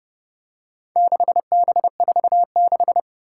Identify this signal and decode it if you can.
Morse code, heard as a keyed tone, about 29 wpm, 700 Hz. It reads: 6B46
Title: Four Challenging Letter-Number Combinations 30wpm